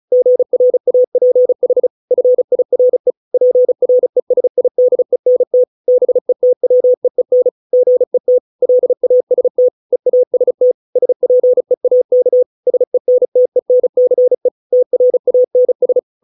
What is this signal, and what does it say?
Morse code, heard as a keyed tone, about 34 wpm, 500 Hz. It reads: GRAPH FIRE PRESIDENT BETWEEN GET LAST EAST SPEAK SENTENCE TRANS